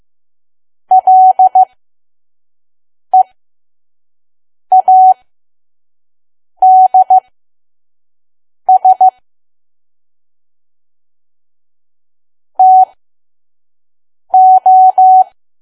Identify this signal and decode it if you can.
Morse code, heard as a keyed tone, about 15 wpm, spaced as for 5 wpm, 750 Hz. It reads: LEADS TO
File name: Archive / 5 WPM files / PM